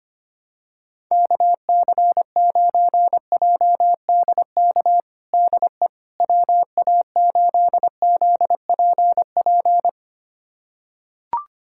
Morse code, heard as a keyed tone, about 25 wpm, 700 Hz. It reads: KC9JDK DE WA8ZPP
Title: USA Callsign DE Callsign - 6 Characters - 25wpm